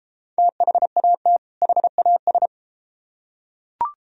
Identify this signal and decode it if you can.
Morse code, heard as a keyed tone, about 33 wpm, 700 Hz. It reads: THAT HAS